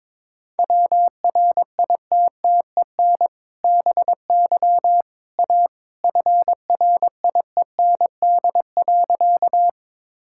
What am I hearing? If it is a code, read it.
Morse code, heard as a keyed tone, about 22 wpm, 700 Hz. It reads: WRITTEN BY A FRIEND.
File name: practice / 22wpm / Sentences with 2-Word Phrases - 22wpm